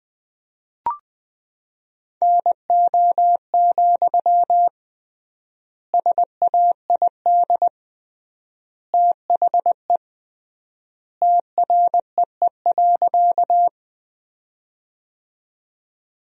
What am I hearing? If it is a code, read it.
Morse code, heard as a keyed tone, about 20 wpm, 700 Hz. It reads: NO, SAID THE TREE.
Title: 3x Word-Spacing - Sentences from Top 300 Words - 20wpm